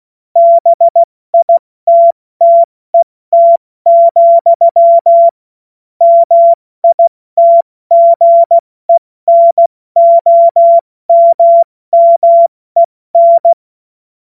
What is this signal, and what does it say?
Morse code, heard as a keyed tone, about 16 wpm, 680 Hz. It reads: BITTET, MITGENOMMEN
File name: Morsekurs / Witze / Witz 31. Mai 2022 70 F80